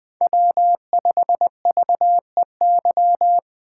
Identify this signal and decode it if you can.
Morse code, heard as a keyed tone, about 20 wpm, 700 Hz. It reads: W5VEY